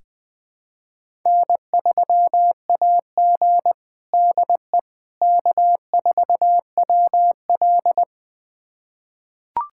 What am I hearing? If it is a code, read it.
Morse code, heard as a keyed tone, about 20 wpm, 700 Hz. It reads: N3AG DE K4WL
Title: USA Callsign DE Callsign - 4 Characters - 20wpm